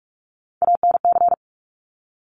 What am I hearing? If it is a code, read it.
Morse code, heard as a keyed tone, about 46 wpm, 700 Hz. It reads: ANC
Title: Top Three Letter Combinations in Words 45wpm